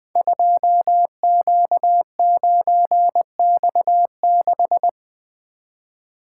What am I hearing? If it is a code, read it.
Morse code, heard as a keyed tone, about 20 wpm, 700 Hz. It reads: 2Q9X6
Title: Code Groups - 5 Letter:Numbers - 20wpm